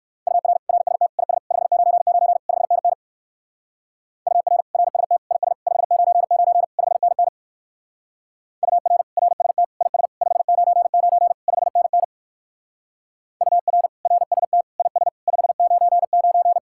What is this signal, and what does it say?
Morse code, heard as a keyed tone, about 59 wpm, 700 Hz. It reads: UR RST IS 599 5NN UR RST IS 599 5NN UR RST IS 599 5NN UR RST IS 599